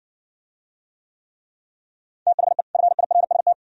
Morse code, heard as a keyed tone, about 60 wpm, 690 Hz. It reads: THE FIRST